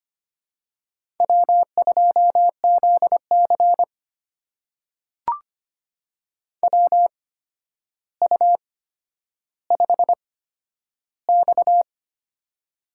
Morse code, heard as a keyed tone, about 25 wpm, spaced as for 11 wpm, 700 Hz. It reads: W2ZC WU5X
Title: USA Call Signs - 4 Characters - 8x Character Spacing - 25wpm